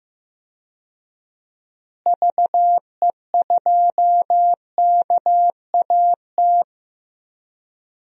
Morse code, heard as a keyed tone, about 15 wpm, 700 Hz. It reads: VE2KAT
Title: DXCC-100 Call Signs - 6 Characters - Repeated 2x - 15wpm